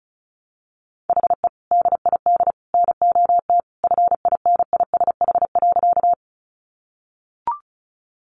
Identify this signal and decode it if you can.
Morse code, heard as a keyed tone, about 34 wpm, 700 Hz. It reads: HE DID NOT FINISH.
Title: Sentences from Top 1200 Words 35wpm